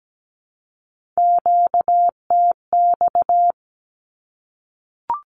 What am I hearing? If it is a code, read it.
Morse code, heard as a keyed tone, about 17 wpm, 700 Hz. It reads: QTX
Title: Q-Codes 17wpm